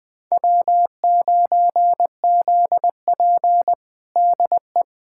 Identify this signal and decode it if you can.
Morse code, heard as a keyed tone, about 20 wpm, 700 Hz. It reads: W9ZP DE
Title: USA Callsign DE Callsign - 4 Characters - 20wpm